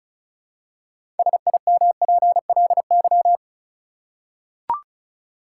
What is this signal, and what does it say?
Morse code, heard as a keyed tone, about 36 wpm, 700 Hz. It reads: SIMPLY
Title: Six Letter Words - 35wpm